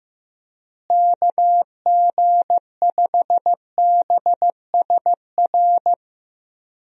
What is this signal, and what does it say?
Morse code, heard as a keyed tone, about 15 wpm, 700 Hz. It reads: KG5BSR
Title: USA Call Signs - 6 Characters - Repeated 3x - 15wpm